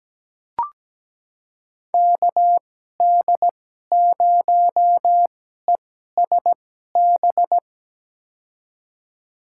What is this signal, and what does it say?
Morse code, heard as a keyed tone, about 17 wpm, spaced as for 12 wpm, 700 Hz. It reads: KD0ESB
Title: USA Call Signs - 2x Character Spacing - 17wpm